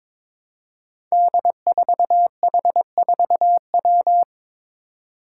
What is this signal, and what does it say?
Morse code, heard as a keyed tone, about 22 wpm, 700 Hz. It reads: D4H4W
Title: Five Challenging Letter-Number Combinations 22wpm